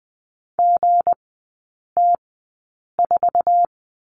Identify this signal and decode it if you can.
Morse code, heard as a keyed tone, about 20 wpm, spaced as for 8 wpm, 700 Hz. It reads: GT4
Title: 166 Lesson 27e - Number 7 - Three Characters 20wpm